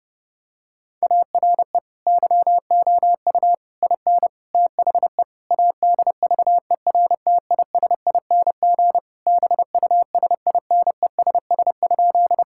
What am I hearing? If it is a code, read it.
Morse code, heard as a keyed tone, about 32 wpm, 700 Hz. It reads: ARE YOU IN THE ADVERTISING BUSINESS?
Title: Sentences from Top 1000 Words - Review Part 6 - 30wpm